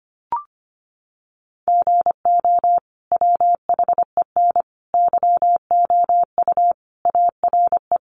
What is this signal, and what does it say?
Morse code, heard as a keyed tone, about 25 wpm, 700 Hz. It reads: GO WHEN YOU ARE
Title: Sentences from Top 500 Words 25wpm